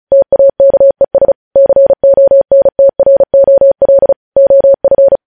Morse code, heard as a keyed tone, about 36 wpm, 550 Hz. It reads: TAKES CONTROL OF
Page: News Headlines in Morse Code at 35 WPM